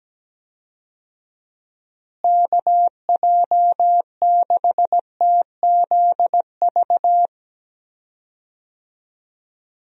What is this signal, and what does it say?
Morse code, heard as a keyed tone, about 17 wpm, 700 Hz. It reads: KJ6TZV
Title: USA Call Signs - 6 Characters - Repeated 2x - 17wpm